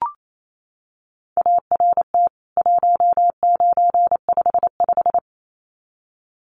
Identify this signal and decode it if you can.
Morse code, heard as a keyed tone, about 28 wpm, 700 Hz. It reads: ART 1955
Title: CWT Exchange 28wpm